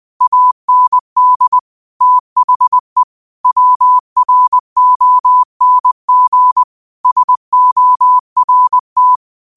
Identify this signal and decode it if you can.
Morse code, heard as a keyed tone, about 20 wpm, 1000 Hz. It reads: AND THE WRONG SORT